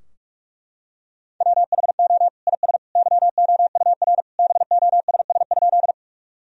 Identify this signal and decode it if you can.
Morse code, heard as a keyed tone, about 42 wpm, 700 Hz. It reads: WHO IS YOUR BOSS?
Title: Sentences from Top 1000 Words - Review Part 4 - 45wpm